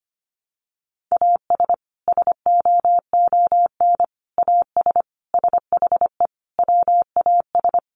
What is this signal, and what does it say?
Morse code, heard as a keyed tone, about 25 wpm, 700 Hz. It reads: AS SOON AS SHE WAS
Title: Sentences from Top 1100 Words 25wpm